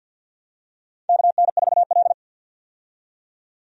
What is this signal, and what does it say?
Morse code, heard as a keyed tone, about 52 wpm, 700 Hz. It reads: KN4L
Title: USA Call Signs - 4 Characters - 50wpm